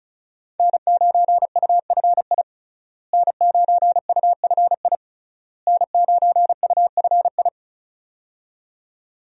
Morse code, heard as a keyed tone, about 36 wpm, 700 Hz. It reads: N9UFI N9UFI N9UFI
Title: USA Call Signs - 5 Characters - Repeated 3x - 35wpm